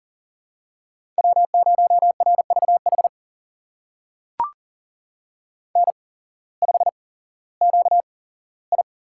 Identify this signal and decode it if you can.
Morse code, heard as a keyed tone, about 40 wpm, spaced as for 19 wpm, 700 Hz. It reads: W0RVH N5QI
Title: USA Call Signs - 8x Character Spacing - 40wpm